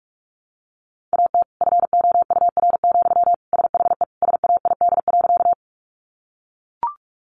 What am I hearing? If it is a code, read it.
Morse code, heard as a keyed tone, about 44 wpm, 700 Hz. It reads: AT FOUR, SHE SAID.
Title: Sentences from Top 300 Words 45wpm